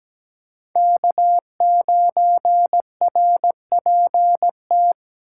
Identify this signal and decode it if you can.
Morse code, heard as a keyed tone, about 17 wpm, 700 Hz. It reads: K9RPT
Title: USA Call Signs - 8x Character Spacing - 17wpm